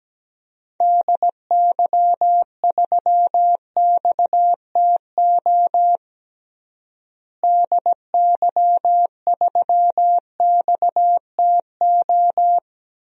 Morse code, heard as a keyed tone, about 17 wpm, 700 Hz. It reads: DY3XTO DY3XTO
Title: DXCC-100 Call Signs - 6 Characters - Repeated 3x - 17wpm